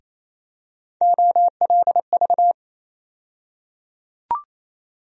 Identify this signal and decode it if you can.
Morse code, heard as a keyed tone, about 28 wpm, 700 Hz. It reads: OLV
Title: USA Call Sign Postfixes - 3 Letters 28wpm